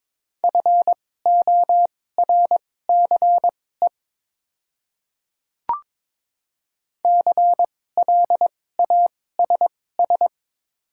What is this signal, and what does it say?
Morse code, heard as a keyed tone, about 22 wpm, spaced as for 16 wpm, 700 Hz. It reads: FORCE CLASS
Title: 2x Character Spacing - Top 500 Words - 22wpm